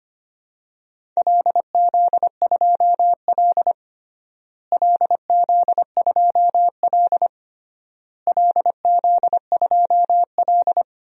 Morse code, heard as a keyed tone, about 25 wpm, 700 Hz. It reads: LZ2L LZ2L LZ2L
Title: DXCC-100 Call Signs - 4 Characters - Repeated 3x - 25wpm